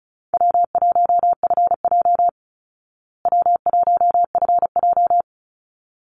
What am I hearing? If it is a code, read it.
Morse code, heard as a keyed tone, about 34 wpm, 700 Hz. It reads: W1FJ W1FJ
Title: Easy Call Signs Repeated 2x - 35wpm